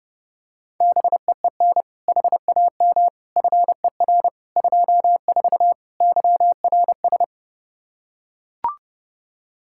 Morse code, heard as a keyed tone, about 30 wpm, 700 Hz. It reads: BEEN HAM FER 24 YRS
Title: Practice QSO - Exchange 3 - More Info - 30wpm